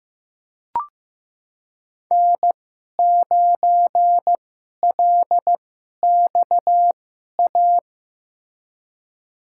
Morse code, heard as a keyed tone, about 15 wpm, spaced as for 11 wpm, 700 Hz. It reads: N9LXA
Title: USA Call Signs - 5 Characters - 2x Character Spacing - 15wpm